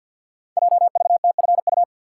Morse code, heard as a keyed tone, about 49 wpm, 700 Hz. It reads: JVTFU
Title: Code Groups - 5 Letter:Numbers - 50wpm